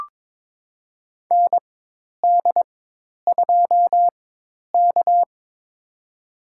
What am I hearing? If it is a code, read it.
Morse code, heard as a keyed tone, about 22 wpm, spaced as for 10 wpm, 700 Hz. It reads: ND2K
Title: USA Call Signs - 4 Characters - 4x Character Spacing - 22wpm